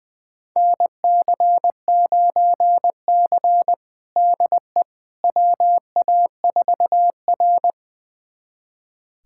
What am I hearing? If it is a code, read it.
Morse code, heard as a keyed tone, about 20 wpm, 700 Hz. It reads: NC9C DE WA4R
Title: USA Callsign DE Callsign - 4 Characters - 20wpm